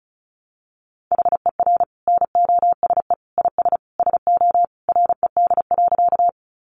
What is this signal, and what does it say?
Morse code, heard as a keyed tone, about 34 wpm, 700 Hz. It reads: HER NOSE IS SO RED.
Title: Easy Phrases 35wpm